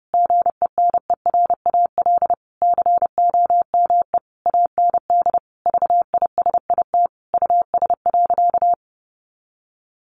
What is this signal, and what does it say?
Morse code, heard as a keyed tone, about 30 wpm, 700 Hz. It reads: GENERAL COME AND VISIT US.
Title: Sentences from Top 1400 Words 30wpm